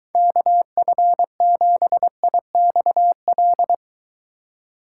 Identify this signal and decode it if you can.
Morse code, heard as a keyed tone, about 23 wpm, 700 Hz. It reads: KF7IXL